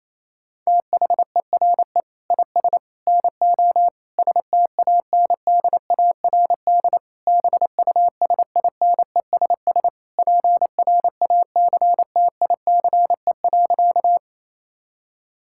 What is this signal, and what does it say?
Morse code, heard as a keyed tone, about 28 wpm, 700 Hz. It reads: THERE IS NO STANDARD BUSINESS PRACTICE.
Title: Sentences from Top 1000 Words - Review Part 6 - 28wpm